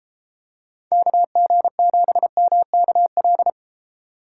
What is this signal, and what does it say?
Morse code, heard as a keyed tone, about 31 wpm, 700 Hz. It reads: KG7MKL